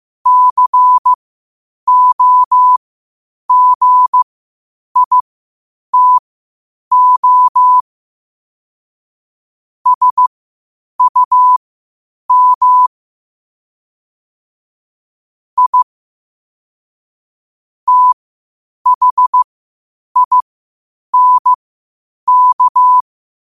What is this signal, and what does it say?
Morse code, heard as a keyed tone, about 15 wpm, spaced as for 8 wpm, 1000 Hz. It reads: COGITO SUM I THINK
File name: qotdfiles / 8wpm